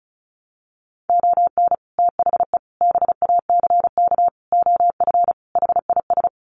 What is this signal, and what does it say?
Morse code, heard as a keyed tone, about 34 wpm, 700 Hz. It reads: ON THE BACK OF HIS
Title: Sets of 5 Words Encore 35wpm